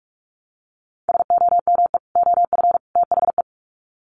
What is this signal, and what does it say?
Morse code, heard as a keyed tone, about 44 wpm, 700 Hz. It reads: SOME OF THE
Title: Sets of 3 Words 45wpm